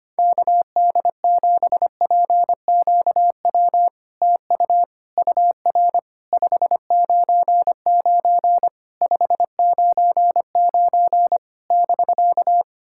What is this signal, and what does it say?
Morse code, heard as a keyed tone, about 25 wpm, 690 Hz. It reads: KD7PQW TU UR 599 599 <BK>
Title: SOTA QSO Practice 25wpm